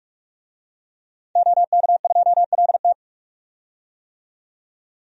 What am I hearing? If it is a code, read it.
Morse code, heard as a keyed tone, about 44 wpm, 700 Hz. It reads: OK2LT
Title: DXCC-100 Call Signs - 5 Characters - 45wpm